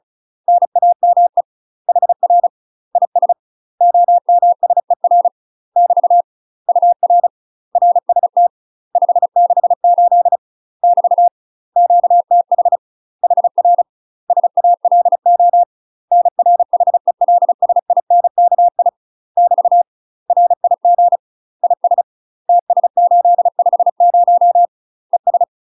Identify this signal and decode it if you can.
Morse code, heard as a keyed tone, about 35 wpm, 700 Hz. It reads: NAME HR IS OMSER <BT> UR RST 568 <BT> QTH HR SALO NR HELSINKI <BT> RIG IS TS850 ES
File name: UK MASTER QSOs TRAINING / 35WPM / UK MASTER QSOs 0037